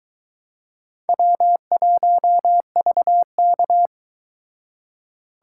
Morse code, heard as a keyed tone, about 23 wpm, 700 Hz. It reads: W1VK